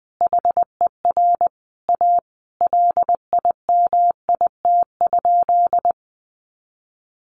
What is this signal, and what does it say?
Morse code, heard as a keyed tone, about 20 wpm, 700 Hz. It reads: HER A LIMIT?
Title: Sentences from Top 1500 Words 20wpm